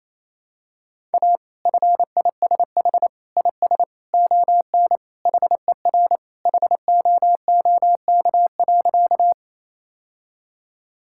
Morse code, heard as a keyed tone, about 28 wpm, 700 Hz. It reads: A FISH IS ON HER HOOK.
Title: Sentences from Top 1600 Words 28wpm